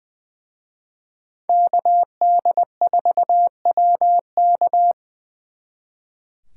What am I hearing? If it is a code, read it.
Morse code, heard as a keyed tone, about 23 wpm, 700 Hz. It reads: KD4WK
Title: Speed Racing - USA 2x2 Call Signs 20wpm